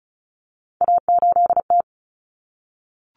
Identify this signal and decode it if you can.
Morse code, heard as a keyed tone, about 34 wpm, 700 Hz. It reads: A8T
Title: Common Cut Numbers - 3-Digit 35wpm